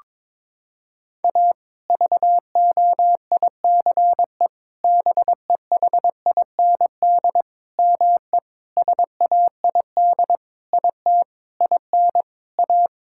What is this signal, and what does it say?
Morse code, heard as a keyed tone, about 21 wpm, 700 Hz. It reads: A VOICE BEHIND ME SAID IT IN A
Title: Sentences from Top 1600 Words 22wpm